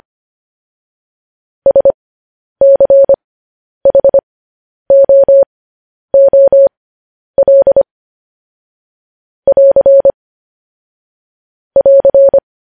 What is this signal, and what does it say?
Morse code, heard as a keyed tone, about 25 wpm, spaced as for 10 wpm, 550 Hz. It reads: SCHOOL <AR> <AR>
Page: news headlines in morse code at 10 wpm